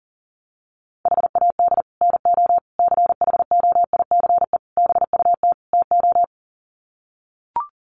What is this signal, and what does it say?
Morse code, heard as a keyed tone, about 40 wpm, 700 Hz. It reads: HAD NO CHOICE BUT TO
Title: Sets of 5 Words 40wpm